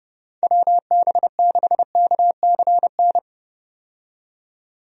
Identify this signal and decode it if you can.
Morse code, heard as a keyed tone, about 30 wpm, 700 Hz. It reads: WB6KCN
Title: USA Call Signs - 6 Characters - 30wpm